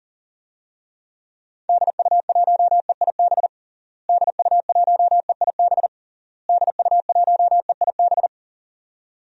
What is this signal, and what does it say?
Morse code, heard as a keyed tone, about 40 wpm, 700 Hz. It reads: DU1EIB DU1EIB DU1EIB
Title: DXCC-100 Call Signs - 6 Characters - Repeated 3x - 40wpm